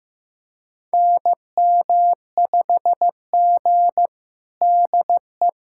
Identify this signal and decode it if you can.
Morse code, heard as a keyed tone, about 15 wpm, 700 Hz. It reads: NM5G DE
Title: USA Callsign DE Callsign - 4 Characters - 15wpm